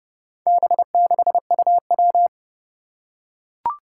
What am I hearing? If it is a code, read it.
Morse code, heard as a keyed tone, about 30 wpm, 700 Hz. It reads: B6UW
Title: Four Challenging Letter-Number Combinations 30wpm